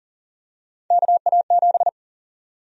Morse code, heard as a keyed tone, about 40 wpm, 700 Hz. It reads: KA7
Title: USA Call Sign Prefixes - 40wpm